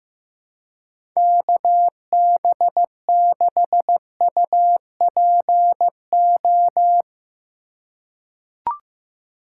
Morse code, heard as a keyed tone, about 15 wpm, 700 Hz. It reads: KB6UPO
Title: USA Call Signs - 6 Characters - 4x Character Spacing - 15wpm